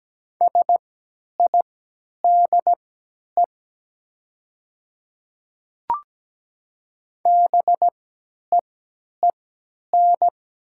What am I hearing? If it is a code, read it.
Morse code, heard as a keyed tone, about 17 wpm, spaced as for 10 wpm, 700 Hz. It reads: SIDE BEEN
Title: 3x Character Spacing - Top 500 Words - 17wpm